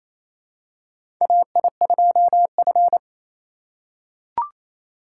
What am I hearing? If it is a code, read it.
Morse code, heard as a keyed tone, about 27 wpm, 700 Hz. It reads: AI2F
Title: USA Call Signs - 4 Characters - 3x Character Spacing - 28wpm